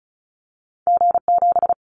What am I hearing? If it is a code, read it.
Morse code, heard as a keyed tone, about 36 wpm, 700 Hz. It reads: G7
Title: Callsign Prefixes for Top 25 DXCC - 35wpm